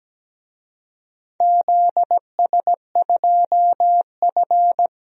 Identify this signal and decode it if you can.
Morse code, heard as a keyed tone, about 17 wpm, 700 Hz. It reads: ZS2F
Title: DXCC-100 Call Signs - 4 Characters - 17wpm